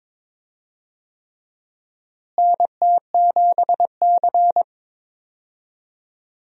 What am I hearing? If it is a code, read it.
Morse code, heard as a keyed tone, about 22 wpm, 700 Hz. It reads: NT7C
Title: USA Call Signs - 4 Characters - Repeated 3x - 22wpm